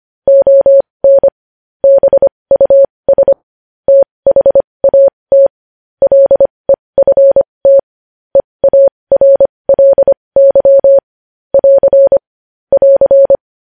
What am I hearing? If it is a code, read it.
Morse code, heard as a keyed tone, about 25 wpm, spaced as for 20 wpm, 550 Hz. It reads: ON BUS THAT LEFT EARLY <AR> <AR>
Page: news headlines in morse code at 20 wpm